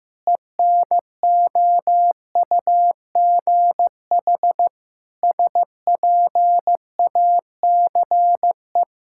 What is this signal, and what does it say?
Morse code, heard as a keyed tone, about 16 wpm, 700 Hz. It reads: ENOUGH SPACE
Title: Sentences from Top 1000 Words - Review Part 5 - 15wpm